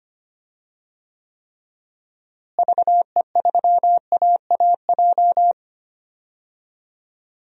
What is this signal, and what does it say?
Morse code, heard as a keyed tone, about 25 wpm, 700 Hz. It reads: VE3AAJ
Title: DXCC-100 Call Signs - 6 Characters - Repeated 3x - 25wpm